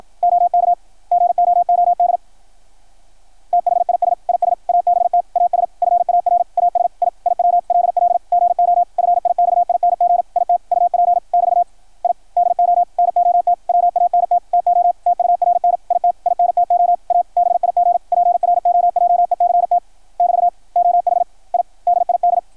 Morse code, heard as a keyed tone, about 53 wpm, 690 Hz, with radio noise. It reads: OK GOOD THIS IS ABT AS FAR AS I IW LL GO FIXING IT UP <BT> I DO NOT WANT TO TURN IT INTO A BIG PROJECT <BT> OH I DID
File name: archived / CRSnet